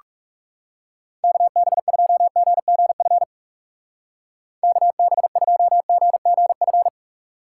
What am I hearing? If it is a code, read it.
Morse code, heard as a keyed tone, about 37 wpm, 700 Hz. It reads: KB2GGF KB2GGF
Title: Speed Racing - USA 2x3 Call Signs 30wpm